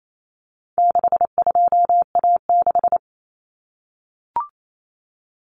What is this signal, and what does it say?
Morse code, heard as a keyed tone, about 28 wpm, 700 Hz. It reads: 62A6
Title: Common Cut Numbers - 4-Digit 28wpm